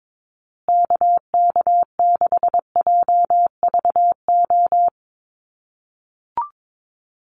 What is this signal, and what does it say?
Morse code, heard as a keyed tone, about 22 wpm, 690 Hz. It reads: KK6JVO
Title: USA Call Signs Repeated 3x 22wpm